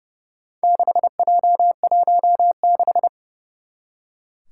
Morse code, heard as a keyed tone, about 30 wpm, 700 Hz. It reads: 6J16
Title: Four Challenging Letter-Number Combinations 30wpm